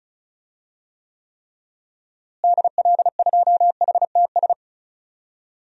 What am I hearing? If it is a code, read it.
Morse code, heard as a keyed tone, about 34 wpm, 700 Hz. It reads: DL2HTS
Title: DXCC-100 Call Signs - 6 Characters - Repeated 3x - 35wpm